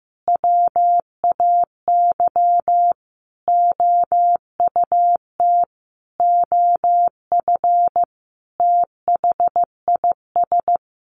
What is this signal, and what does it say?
Morse code, heard as a keyed tone, about 15 wpm, 700 Hz. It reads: WAY OUT OF THIS
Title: Sentences from Top 1200 Words 15wpm